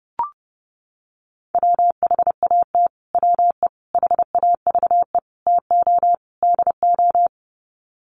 Morse code, heard as a keyed tone, about 30 wpm, 700 Hz. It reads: WHAT WE HAVE TO DO
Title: Sets of 5 Words Encore 30wpm